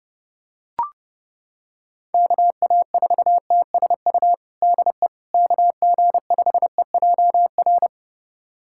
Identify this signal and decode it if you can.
Morse code, heard as a keyed tone, about 29 wpm, 700 Hz. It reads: KA4TSU DE KG5EJR
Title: USA Callsign DE Callsign - 6 Characters - 30wpm